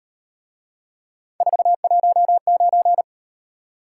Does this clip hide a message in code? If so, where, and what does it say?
Yes: Morse code from 1.4 to 3.0 s, heard as a keyed tone, about 37 wpm, 700 Hz. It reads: 419